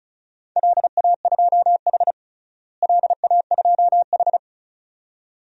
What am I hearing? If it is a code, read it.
Morse code, heard as a keyed tone, about 34 wpm, 700 Hz. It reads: LA2H LA2H
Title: DXCC-100 Call Signs - 4 Characters - Repeated 2x - 35wpm